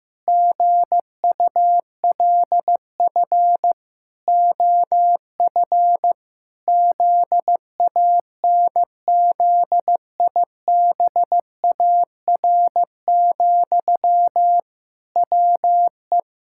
Morse code, heard as a keyed tone, about 15 wpm, 700 Hz. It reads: GULF OF ZANZIBAR, WE
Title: Pangram Challenge v2 - 15wpm